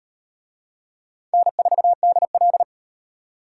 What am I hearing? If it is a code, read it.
Morse code, heard as a keyed tone, about 34 wpm, 700 Hz. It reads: N4DL